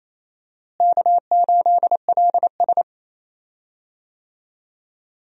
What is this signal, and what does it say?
Morse code, heard as a keyed tone, about 28 wpm, 700 Hz. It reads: K8LS
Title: USA Call Signs - 4 Characters - Repeated 2x - 28wpm